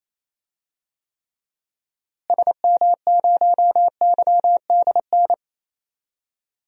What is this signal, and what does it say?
Morse code, heard as a keyed tone, about 27 wpm, 700 Hz. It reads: SM0YDN
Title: DXCC-100 Call Signs - 6 Characters - Repeated 2x - 28wpm